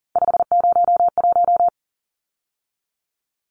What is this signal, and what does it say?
Morse code, heard as a keyed tone, about 40 wpm, 700 Hz. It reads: 501
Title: Numbers - 3-Digit 40wpm